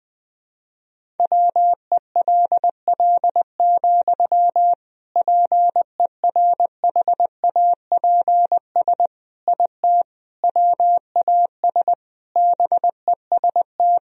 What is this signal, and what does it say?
Morse code, heard as a keyed tone, about 20 wpm, 700 Hz. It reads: WELL, PERHAPS IT WAS BEST
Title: Sentences from Top 500 Words and Abbreviations 20wpm